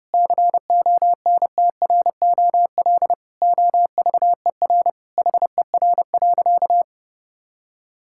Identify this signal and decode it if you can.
Morse code, heard as a keyed tone, about 32 wpm, 700 Hz. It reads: CONTROL OVER HER.
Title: Sentences from Top 1000 Words - Review Part 5 - 30wpm